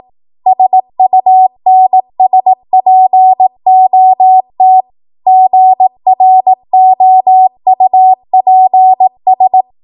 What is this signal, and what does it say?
Morse code, heard as a keyed tone, about 18 wpm, 750 Hz. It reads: SUNSPOT GROUPS
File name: Archive / 18 WPM / 18WPM